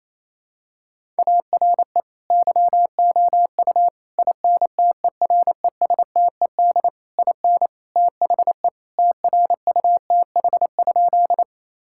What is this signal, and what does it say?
Morse code, heard as a keyed tone, about 28 wpm, 700 Hz. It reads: ARE YOU INTERESTED IN THE TRUTH?
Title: Sentences from Top 1000 Words - Review Part 6 - 28wpm